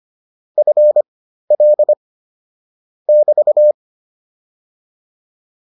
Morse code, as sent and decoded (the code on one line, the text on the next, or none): ..-. .-.. / -...-
FL <BT>